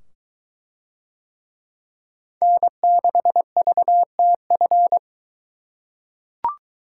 Morse code, heard as a keyed tone, about 23 wpm, 700 Hz. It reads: N6VTF